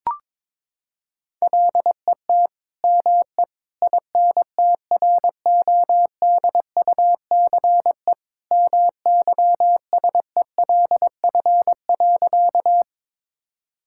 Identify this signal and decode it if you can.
Morse code, heard as a keyed tone, about 22 wpm, 700 Hz. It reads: LET ME INTRODUCE MYSELF.
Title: Sentences from Top 1600 Words 22wpm